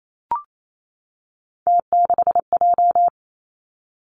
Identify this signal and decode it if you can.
Morse code, heard as a keyed tone, about 28 wpm, 700 Hz. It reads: T6J
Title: Three Challenging Letter-Number Combinations 28wpm